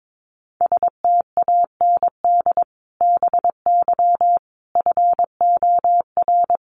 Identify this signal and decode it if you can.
Morse code, heard as a keyed tone, about 22 wpm, 700 Hz. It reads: STAND BY FOR
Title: Sentences from Top 1500 Words 22wpm